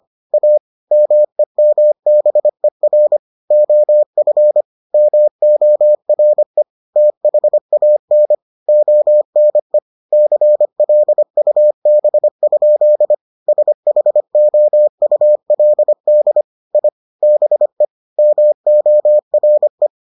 Morse code, heard as a keyed tone, about 25 wpm, 600 Hz. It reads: A MEMBER OF MORE THAN ONE CLUB? SHOULD I BE MORE